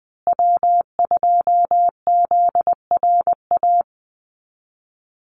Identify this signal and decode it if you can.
Morse code, heard as a keyed tone, about 19 wpm, 700 Hz. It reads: W2ZRA
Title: Call Signs Repeated 2x - 20wpm